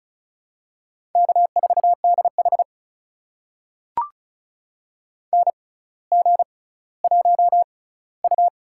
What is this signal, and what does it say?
Morse code, heard as a keyed tone, about 36 wpm, spaced as for 18 wpm, 700 Hz. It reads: K4DH NG1U
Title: USA Call Signs - 4 Characters - 6x Character Spacing - 35wpm